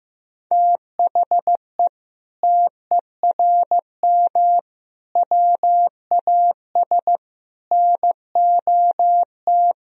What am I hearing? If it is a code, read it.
Morse code, heard as a keyed tone, about 15 wpm, 700 Hz. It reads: THE TERM WAS NOT
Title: Sentences from Top 1000 Words - Review Part 4 - 15wpm